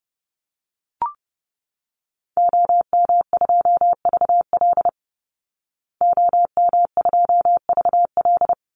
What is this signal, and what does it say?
Morse code, heard as a keyed tone, about 30 wpm, 700 Hz. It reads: OM2VL OM2VL
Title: Call Signs Repeated 3x - 30wpm